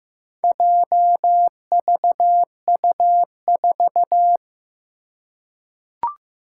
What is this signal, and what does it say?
Morse code, heard as a keyed tone, about 15 wpm, 700 Hz. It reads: JVU4
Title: Four Challenging Letter-Number Combinations 15wpm